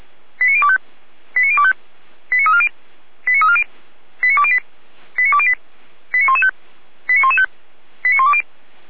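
Five-tone selective call (ZVEI), heard as quick runs of five tones, tones of about 70 ms. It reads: 89025 89025 89340 89340 89208 89208 89115 89115 89120
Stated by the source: ZVEI five-tone selective call received by SDR.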